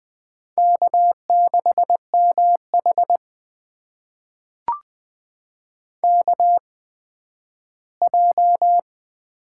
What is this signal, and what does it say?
Morse code, heard as a keyed tone, about 20 wpm, spaced as for 12 wpm, 700 Hz. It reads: K6MH KJ
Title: USA Call Signs - 8x Character Spacing - 20wpm